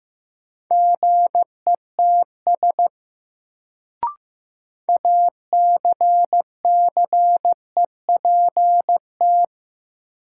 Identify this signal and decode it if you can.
Morse code, heard as a keyed tone, about 15 wpm, 700 Hz. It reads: GETS ACCEPT
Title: Sentences from Top 1300 Words - New Word Drill Alt - 15wpm